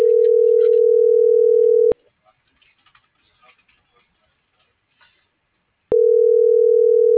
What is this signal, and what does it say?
Ringback tone (North America), 440 + 480 Hz.
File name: CENTRAL MAERSK / out sandovalmsk - .69352